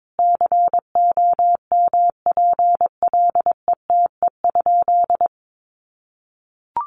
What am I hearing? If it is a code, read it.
Morse code, heard as a keyed tone, about 22 wpm, 700 Hz. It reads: COMPLETE?
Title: Sentences from Top 1500 Words 22wpm